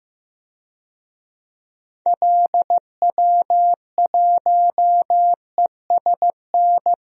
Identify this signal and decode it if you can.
Morse code, heard as a keyed tone, about 15 wpm, 700 Hz. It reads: LW1ESN
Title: DXCC-100 Call Signs - 6 Characters - 15wpm